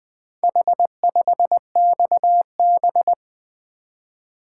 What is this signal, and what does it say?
Morse code, heard as a keyed tone, about 20 wpm, 700 Hz. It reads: H5XB
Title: Four Letter-Number Combinations - 20wpm